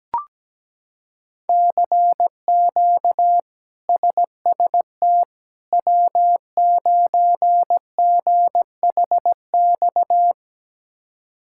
Morse code, heard as a keyed tone, about 17 wpm, 700 Hz. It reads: CQ SST W9GHX
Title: K1USN SST QSO Practice - 17wpm